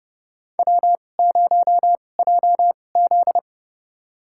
Morse code, heard as a keyed tone, about 29 wpm, spaced as for 21 wpm, 700 Hz. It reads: W0JZ
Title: USA Call Signs - 4 Characters - 2x Character Spacing - 30wpm